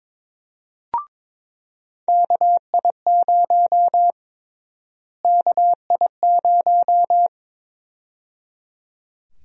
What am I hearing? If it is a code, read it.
Morse code, heard as a keyed tone, about 21 wpm, 700 Hz. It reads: KI0 KI0
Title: USA Call Sign Prefixes - Repeated 2x - 22wpm